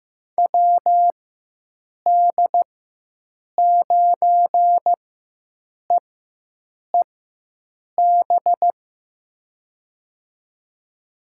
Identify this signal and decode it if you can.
Morse code, heard as a keyed tone, about 15 wpm, spaced as for 7 wpm, 700 Hz. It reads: WD9EEB
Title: USA Call Signs - 6 Characters - 4x Character Spacing - 15wpm